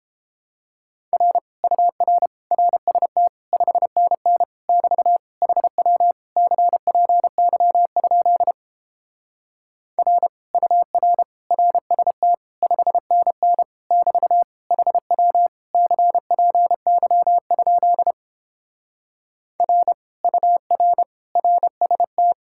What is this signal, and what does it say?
Morse code, heard as a keyed tone, about 29 wpm, 700 Hz. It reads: R UR RST 5NN <BT> HW CPY? R UR RST 5NN <BT> HW CPY? R UR RST